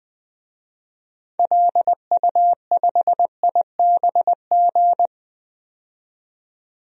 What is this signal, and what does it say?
Morse code, heard as a keyed tone, about 20 wpm, 700 Hz. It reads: LU5IBG
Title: DXCC-100 Call Signs - 6 Characters - 20wpm